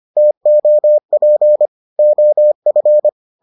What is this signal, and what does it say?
Morse code, heard as a keyed tone, about 25 wpm, 600 Hz. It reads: TOP OF